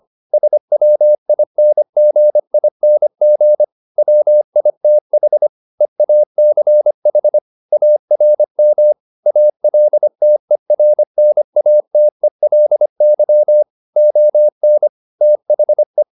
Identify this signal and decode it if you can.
Morse code, heard as a keyed tone, about 25 wpm, 600 Hz. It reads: SWINGING WITH EACH ARM ALTERNATELY ON THE
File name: the old man and the sea part 1 25wpm